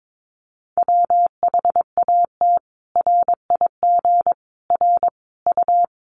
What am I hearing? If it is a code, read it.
Morse code, heard as a keyed tone, about 22 wpm, 700 Hz. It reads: WHAT RIG R U
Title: Common QSO Phrases 22wpm